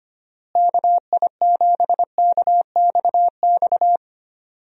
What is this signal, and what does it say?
Morse code, heard as a keyed tone, about 25 wpm, 700 Hz. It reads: KI7KXX
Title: USA Call Signs - 6 Characters - Repeated 3x - 25wpm